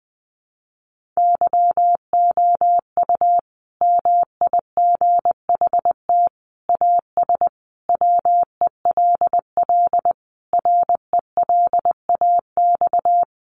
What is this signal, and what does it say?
Morse code, heard as a keyed tone, about 20 wpm, 700 Hz. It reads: YOU MIGHT AS WELL RELAX
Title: Sentences from Top 1400 Words 20wpm